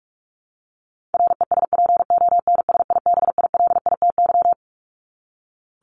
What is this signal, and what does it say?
Morse code, heard as a keyed tone, about 44 wpm, 700 Hz. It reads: RESPONSIBILITY
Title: Top 1001-1250 Words 45wpm